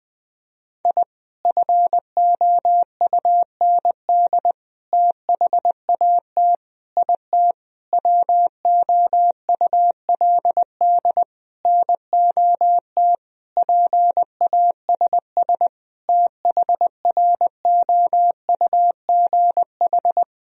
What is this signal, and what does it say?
Morse code, heard as a keyed tone, about 21 wpm, 700 Hz. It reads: I FOUND THAT IT WOULD NOT PASS THROUGH
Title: Sentences from Top 1000 Words - Review Part 1 - 20wpm